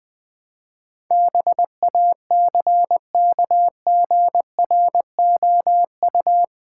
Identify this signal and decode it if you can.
Morse code, heard as a keyed tone, about 20 wpm, 700 Hz. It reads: BACKGROU
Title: Build A Word - 4 Letters At A Time - 10-Letter Words - 20wpm